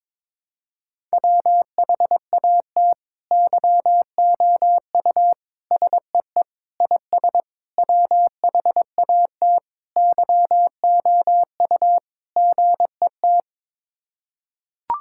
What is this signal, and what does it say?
Morse code, heard as a keyed tone, about 21 wpm, 700 Hz. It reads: WHAT YOU SEE IS WHAT YOU GET
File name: practice / 22wpm / Sets of 7 Words - 22wpm